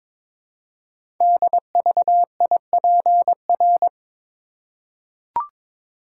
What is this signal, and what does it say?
Morse code, heard as a keyed tone, about 22 wpm, 700 Hz. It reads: DVIPR
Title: Code Groups - 5 Letters - 22wpm